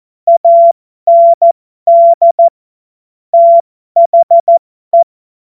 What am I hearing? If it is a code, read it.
Morse code, heard as a keyed tone, about 14 wpm, spaced as for 12 wpm, 680 Hz. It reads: AND THE
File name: Morsekurs / Witze